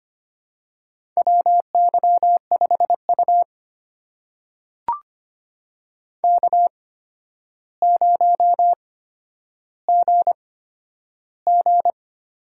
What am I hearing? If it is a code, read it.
Morse code, heard as a keyed tone, about 25 wpm, spaced as for 11 wpm, 700 Hz. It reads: WY5U K0GG
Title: USA Call Signs - 4 Characters - 8x Character Spacing - 25wpm